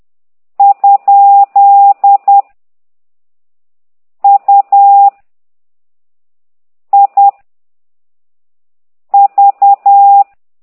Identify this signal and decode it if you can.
Morse code, heard as a keyed tone, about 10 wpm, 800 Hz. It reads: ? U I V
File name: ct2idw / exerc15